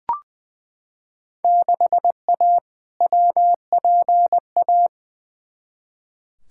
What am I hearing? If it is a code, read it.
Morse code, heard as a keyed tone, about 20 wpm, 700 Hz. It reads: 6A WPA
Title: Field Day Exchanges Encore - 20wpm